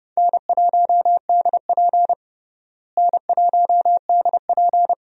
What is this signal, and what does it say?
Morse code, heard as a keyed tone, about 30 wpm, 700 Hz. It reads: N1DP N1DP
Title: USA Call Signs - 4 Characters - Repeated 2x - 30wpm